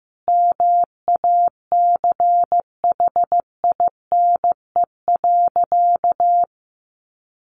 Morse code, heard as a keyed tone, about 15 wpm, 700 Hz. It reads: MACHINE.
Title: A Sound Of Thunder 15wpm